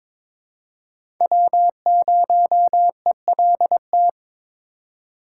Morse code, heard as a keyed tone, about 22 wpm, 700 Hz. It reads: W0ELT
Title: USA Call Signs - 5 Characters - 3x Character Spacing - 22wpm